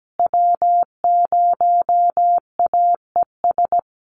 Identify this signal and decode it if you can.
Morse code, heard as a keyed tone, about 17 wpm, 700 Hz. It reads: W0AES
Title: USA Call Signs Repeated 3x 17wpm